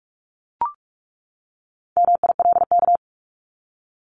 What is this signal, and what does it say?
Morse code, heard as a keyed tone, about 44 wpm, 700 Hz. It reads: MILK
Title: Top 1251-1500 Words 45wpm